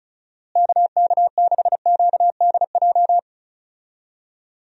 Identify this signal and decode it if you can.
Morse code, heard as a keyed tone, about 36 wpm, 700 Hz. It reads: KK6QDJ
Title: USA Call Signs - 6 Characters - Repeated 3x - 35wpm